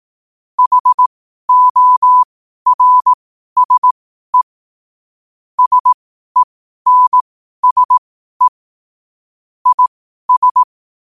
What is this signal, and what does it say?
Morse code, heard as a keyed tone, about 18 wpm, spaced as for 12 wpm, 1000 Hz. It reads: HORSE SENSE IS
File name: qotdfiles / 12wpm